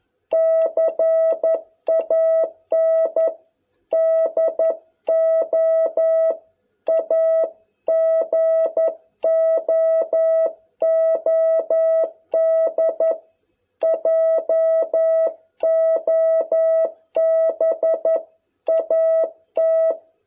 Morse code, heard as a keyed tone, about 11 wpm, 660 Hz. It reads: CAN DO A GOOD JOB AT